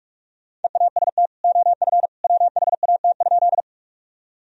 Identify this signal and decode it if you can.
Morse code, heard as a keyed tone, about 41 wpm, 700 Hz. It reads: EAST OF WHAT?
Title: Sentences from Top 1000 Words - Review Part 3 - 45wpm